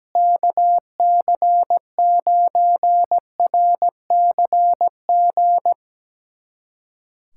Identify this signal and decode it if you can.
Morse code, heard as a keyed tone, about 17 wpm, 700 Hz. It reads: KC9RCG
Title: USA Callsign DE Callsign - 6 Characters - 17wpm